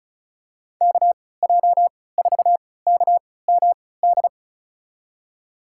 Morse code, heard as a keyed tone, about 34 wpm, spaced as for 20 wpm, 700 Hz. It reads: KJ4KMD
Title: USA Call Signs - 6 Characters - 3x Character Spacing - 35wpm